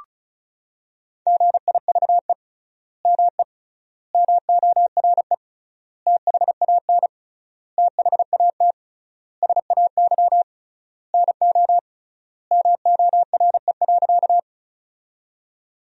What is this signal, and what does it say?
Morse code, heard as a keyed tone, about 36 wpm, 700 Hz. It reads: GIVE ME MORE THAN THAT SAY NO MORE.
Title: 3x Word-Spacing - Sentences from Top 200 Words - 35wpm